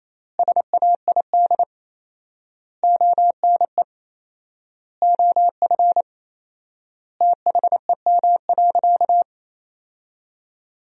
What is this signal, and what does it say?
Morse code, heard as a keyed tone, about 28 wpm, 700 Hz. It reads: SAID ONE OF THEM.
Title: 4x Word-Spacing - Sentences from Top 300 Words - 28wpm